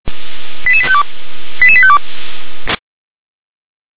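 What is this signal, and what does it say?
Five-tone selective call (ZVEI), heard as quick runs of five tones, tones of about 70 ms. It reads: 88852 88852
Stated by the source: ZVEI five-tone selective call received by SDR.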